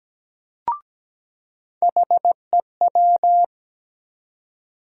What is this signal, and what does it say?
Morse code, heard as a keyed tone, about 17 wpm, 700 Hz. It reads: HEW
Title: USA Call Sign Postfixes - 3 Letters 17wpm